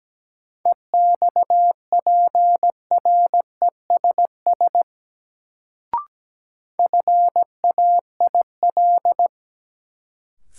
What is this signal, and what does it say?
Morse code, heard as a keyed tone, about 17 wpm, 700 Hz. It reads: EXPRESS FAIL
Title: Sentences from Top 1100 Words - New Word Drill Alt - 17wpm